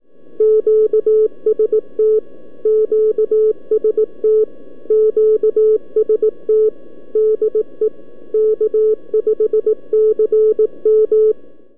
Morse code, heard as a keyed tone, about 18 wpm, 430 Hz. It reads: QST QST QST DE K5CM
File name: k5cm 20MornCallup East-by abi